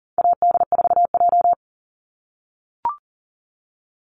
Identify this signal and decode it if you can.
Morse code, heard as a keyed tone, about 40 wpm, 700 Hz. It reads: AD4J
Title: Call Signs - 40wpm